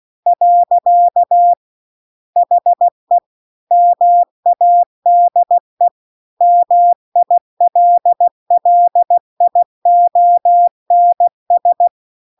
Morse code, heard as a keyed tone, about 16 wpm, 700 Hz. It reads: . HE MADE MILLIONS